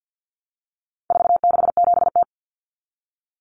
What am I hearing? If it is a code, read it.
Morse code, heard as a keyed tone, about 50 wpm, 700 Hz. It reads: 467T
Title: Common Cut Numbers - 4-Digit 50wpm